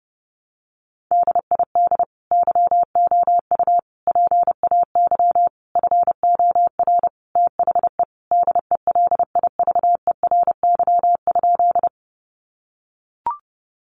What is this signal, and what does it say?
Morse code, heard as a keyed tone, about 30 wpm, 700 Hz. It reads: DID YOU PAY FOR THE DELIVERY?
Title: Sentences from Top 1500 Words 30wpm